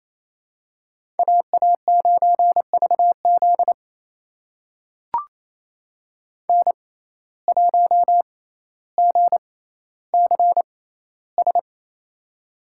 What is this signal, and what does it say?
Morse code, heard as a keyed tone, about 28 wpm, spaced as for 14 wpm, 700 Hz. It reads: AA9VZ N1GCS
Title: USA Call Signs - 6x Character Spacing - 28wpm